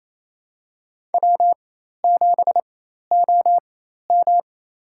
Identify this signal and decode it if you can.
Morse code, heard as a keyed tone, about 28 wpm, spaced as for 13 wpm, 700 Hz. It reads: W7OM
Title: USA Call Signs - 4 Characters - 4x Character Spacing - 28wpm